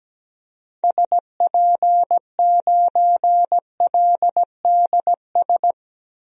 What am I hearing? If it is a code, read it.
Morse code, heard as a keyed tone, about 17 wpm, 700 Hz. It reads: SP9LDS
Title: DXCC-100 Call Signs - 6 Characters - Repeated 2x - 17wpm